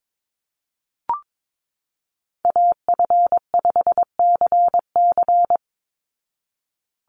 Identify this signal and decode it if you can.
Morse code, heard as a keyed tone, about 22 wpm, 700 Hz. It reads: AF5CC
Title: Call Signs Repeated 3x - 22wpm